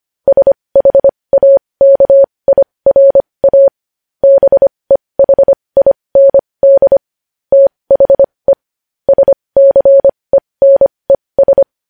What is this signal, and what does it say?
Morse code, heard as a keyed tone, about 25 wpm, spaced as for 20 wpm, 550 Hz. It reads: SHAKIRA BEHIND THE SCENES
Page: News headlines in Morse code at 20 wpm